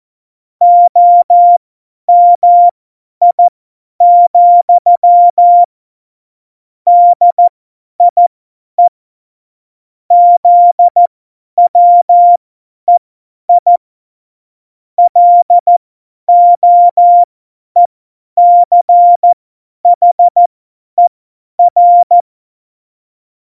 Morse code, heard as a keyed tone, about 14 wpm, spaced as for 10 wpm, 700 Hz. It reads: OMI, DIE ZWEI LOECHER
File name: Morsekurs / Witze / Witz 17. Oktober 17, Tempo 50